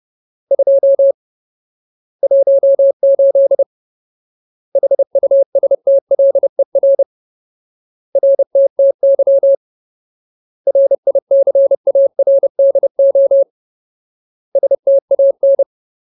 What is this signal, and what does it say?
Morse code, heard as a keyed tone, about 30 wpm, 550 Hz. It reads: 2 18 HUSTLER RTTY RICARDO STAN
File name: Ultimate Word List / 30WPM Ultimate